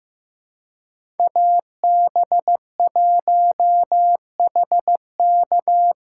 Morse code, heard as a keyed tone, about 15 wpm, 700 Hz. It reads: AB1HK
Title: USA Callsign DE Callsign - 5 Characters - 15wpm